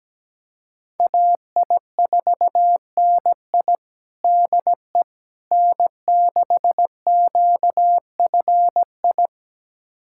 Morse code, heard as a keyed tone, about 17 wpm, 700 Hz. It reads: AI4NI DE N6QFI
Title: USA Callsign DE Callsign - 5 Characters - 17wpm